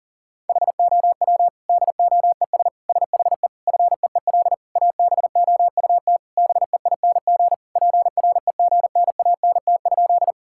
Morse code, heard as a keyed tone, about 40 wpm, 700 Hz. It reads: HOW DOES SHE FEEL ABOUT BEING PREGNANT?
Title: Sentences from Top 1000 Words - Review Part 6 - 40wpm